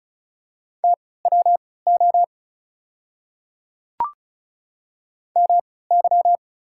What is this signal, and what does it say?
Morse code, heard as a keyed tone, about 36 wpm, spaced as for 20 wpm, 690 Hz. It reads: TWO MY
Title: 3x Character Spacing - Top 100 Words - 35wpm